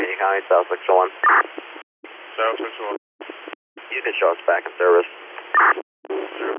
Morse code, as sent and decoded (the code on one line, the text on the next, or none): none